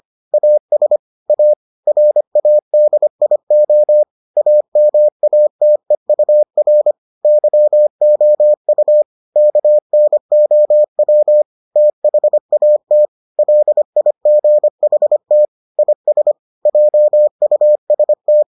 Morse code, heard as a keyed tone, about 25 wpm, 600 Hz. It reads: AS A RADIO AMATEUR YOU KNOW THAT LIGHT IS JUST